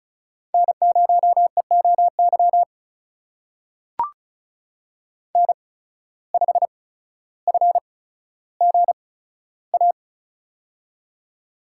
Morse code, heard as a keyed tone, about 36 wpm, spaced as for 15 wpm, 700 Hz. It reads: N0EOY N5FGA
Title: USA Call Signs - 5 Characters - 8x Character Spacing - 35wpm